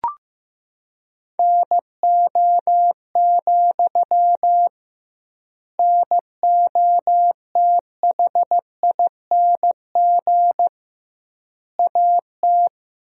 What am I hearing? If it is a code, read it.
Morse code, heard as a keyed tone, about 15 wpm, 700 Hz. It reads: NO, NOTHING AT
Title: 2x Word-Spacing - Sentences from Top 500 Words - 15wpm